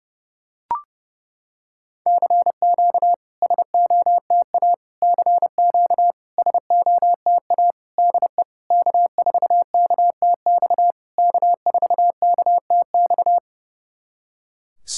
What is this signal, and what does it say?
Morse code, heard as a keyed tone, about 30 wpm, 700 Hz. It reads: CQ SOTA CQ SOTA DE K4KTX K4KTX
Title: SOTA QSO Practice 30wpm